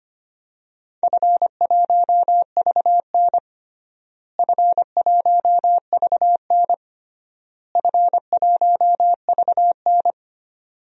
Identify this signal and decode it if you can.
Morse code, heard as a keyed tone, about 25 wpm, 700 Hz. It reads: F1VN F1VN F1VN
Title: DXCC-100 Call Signs - 4 Characters - Repeated 3x - 25wpm